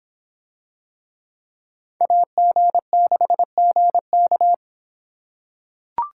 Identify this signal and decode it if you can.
Morse code, heard as a keyed tone, about 26 wpm, 700 Hz. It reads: AG6GK